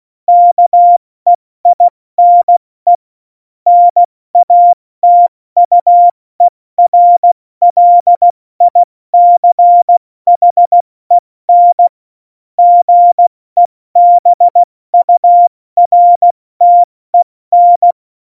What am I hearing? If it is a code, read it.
Morse code, heard as a keyed tone, about 16 wpm, 700 Hz. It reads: KEINE NATUERLICHEN GEBURTEN
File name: Morsekurs / Witze / Witz 19. Februar 2019, 70 F80